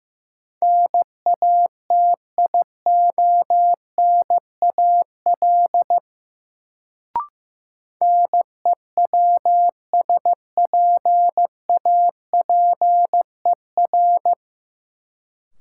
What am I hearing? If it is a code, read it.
Morse code, heard as a keyed tone, about 15 wpm, 700 Hz. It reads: NATIONAL NEWSPAPER
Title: Sentences from Top 1000 Words - Review Part 4 - New Word Drill - 15wpm